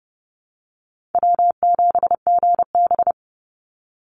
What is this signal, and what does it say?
Morse code, heard as a keyed tone, about 30 wpm, 700 Hz. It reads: W7GB
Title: Easy Call Signs Repeated 3x - 30wpm